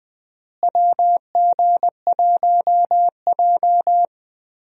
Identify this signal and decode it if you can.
Morse code, heard as a keyed tone, about 20 wpm, 700 Hz. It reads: WG1J
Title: USA Call Signs - 4 Characters - Repeated 3x - 20wpm